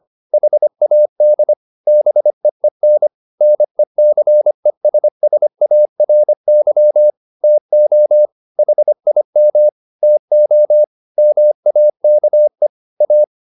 Morse code, heard as a keyed tone, about 25 wpm, 600 Hz. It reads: HAD BEEN NECESSARY TO HIM TO MAKE A
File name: the old man and the sea part 1 25wpm